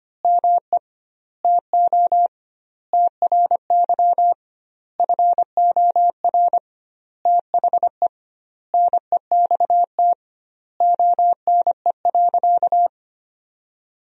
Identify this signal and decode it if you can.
Morse code, heard as a keyed tone, about 25 wpm, 700 Hz. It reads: ME TO TRY FOR THE NEXT ONE.
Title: 2x Word-Spacing - Sentences from Top 400 Words - 25wpm